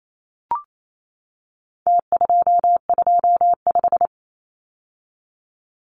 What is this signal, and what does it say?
Morse code, heard as a keyed tone, about 28 wpm, 700 Hz. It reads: T225
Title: Numbers - Contest Style - 4-Digit 28wpm